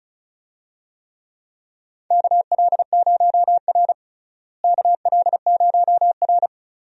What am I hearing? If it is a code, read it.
Morse code, heard as a keyed tone, about 36 wpm, 700 Hz. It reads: KL0R KL0R
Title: USA Call Signs - 4 Characters - Repeated 2x - 35wpm